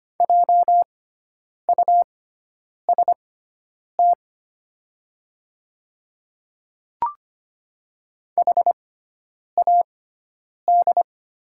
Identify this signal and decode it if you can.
Morse code, heard as a keyed tone, about 25 wpm, spaced as for 9 wpm, 700 Hz. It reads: JUST HAD
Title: 6x Character Spacing - Top 300 Words - 25wpm